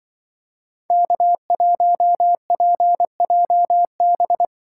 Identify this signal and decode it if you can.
Morse code, heard as a keyed tone, about 24 wpm, 700 Hz. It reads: K1PJB
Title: Speed Racing - USA 1x3 Call Signs 20wpm